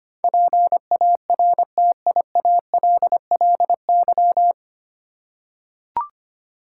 Morse code, heard as a keyed tone, about 25 wpm, 700 Hz. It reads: PARTIALLY
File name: Nine Letter Words - 25wpm